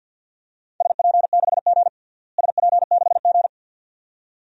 Morse code, heard as a keyed tone, about 46 wpm, 700 Hz. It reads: SP6G SP6G
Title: DXCC-100 Call Signs - 4 Characters - Repeated 2x - 50wpm